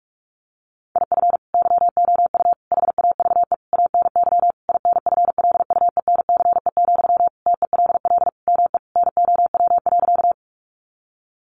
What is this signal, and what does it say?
Morse code, heard as a keyed tone, about 44 wpm, 700 Hz. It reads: IF YOU HAVE ANY INFLUENCE, TELL ME NOW.
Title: Sentences from Top 1100 Words 45wpm